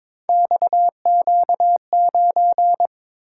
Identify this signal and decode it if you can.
Morse code, heard as a keyed tone, about 22 wpm, 700 Hz. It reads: XQ9
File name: practice / 22wpm / Three Letter-Number Combinations - 22wpm